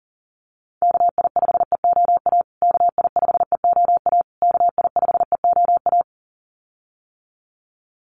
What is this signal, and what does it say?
Morse code, heard as a keyed tone, about 40 wpm, 700 Hz. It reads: KI5EOA KI5EOA KI5EOA
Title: USA Call Signs Repeated 3x 40wpm